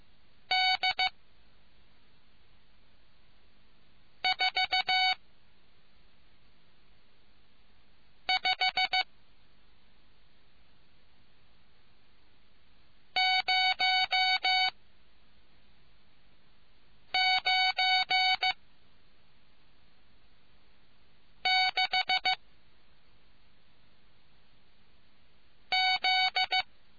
Morse code, heard as a keyed tone, about 15 wpm, 750 Hz. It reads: D 4 5 0 9 6 Z